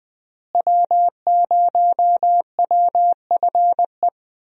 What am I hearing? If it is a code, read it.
Morse code, heard as a keyed tone, about 20 wpm, 700 Hz. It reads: W0WFE
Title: USA Call Signs - 5 Characters - Repeated 3x - 20wpm